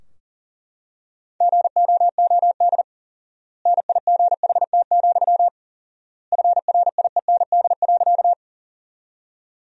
Morse code, heard as a keyed tone, about 40 wpm, 700 Hz. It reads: GOOD NIGHT, FRIEND.
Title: 4x Word-Spacing - Sentences from Top 300 Words - 40wpm